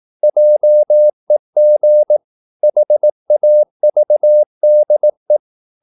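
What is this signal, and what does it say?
Morse code, heard as a keyed tone, about 18 wpm, 600 Hz. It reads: JEG HAVDE
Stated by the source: Slow (QRS) Morse practice at 18 wpm, CW med Gnister.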